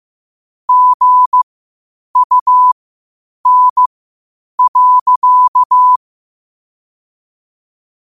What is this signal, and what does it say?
Morse code, heard as a keyed tone, about 15 wpm, spaced as for 8 wpm, 1000 Hz. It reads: GUN.
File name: qotdfiles / 8wpm